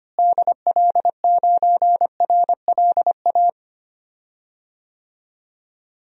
Morse code, heard as a keyed tone, about 25 wpm, 700 Hz. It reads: DL9RLA
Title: DXCC-100 Call Signs - 6 Characters - Repeated 3x - 25wpm